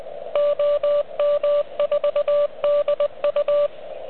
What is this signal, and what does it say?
Morse code, heard as a keyed tone, about 21 wpm, 590 Hz. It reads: OM4DU
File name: OK QRP 2022 / Zvuk / REC37